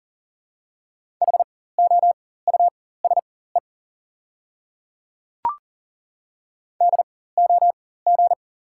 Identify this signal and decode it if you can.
Morse code, heard as a keyed tone, about 40 wpm, spaced as for 19 wpm, 690 Hz. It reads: HOUSE DOG
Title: 4x Character Spacing - Top 500 Words - 40wpm